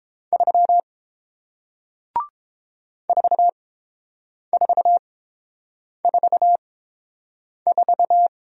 Morse code, heard as a keyed tone, about 30 wpm, spaced as for 7 wpm, 700 Hz. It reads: 3 4444